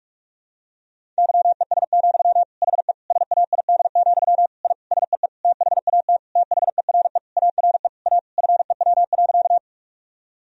Morse code, heard as a keyed tone, about 41 wpm, 700 Hz. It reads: YES, HE SAID, I SEE THAT THERE ARE A FEW.
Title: Sentences from Top 1000 Words - Review Part 1 - 45wpm